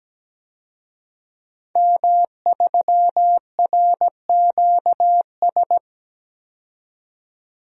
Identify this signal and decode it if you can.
Morse code, heard as a keyed tone, about 17 wpm, 700 Hz. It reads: M3RQS
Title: DXCC-100 Call Signs - 5 Characters - 17wpm